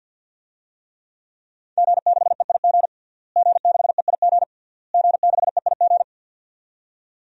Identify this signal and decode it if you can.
Morse code, heard as a keyed tone, about 52 wpm, 700 Hz. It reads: G6EIG G6EIG G6EIG
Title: DXCC-100 Call Signs - 5 Characters - Repeated 3x - 50wpm